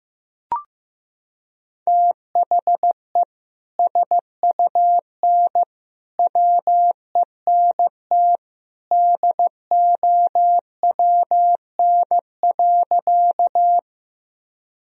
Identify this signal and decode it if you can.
Morse code, heard as a keyed tone, about 15 wpm, 700 Hz. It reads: THE SUN WENT DOWN.
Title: Sentences from Top 1000 Words - Review Part 1 - 15wpm